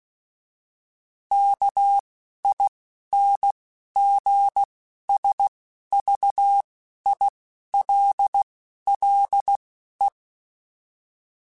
Morse code, heard as a keyed tone, about 16 wpm, spaced as for 11 wpm, 800 Hz. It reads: KINGSVILLE
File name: fromSMs / FromSMs mp3s / Cities@10